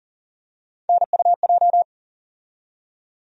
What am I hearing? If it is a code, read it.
Morse code, heard as a keyed tone, about 40 wpm, 700 Hz. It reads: NUJ
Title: USA Call Sign Postfixes - 3 Letters 40wpm